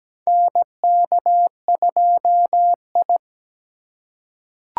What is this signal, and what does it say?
Morse code, heard as a keyed tone, about 17 wpm, 700 Hz. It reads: NK2I
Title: USA Call Signs - 4 Characters - 3x Character Spacing - 17wpm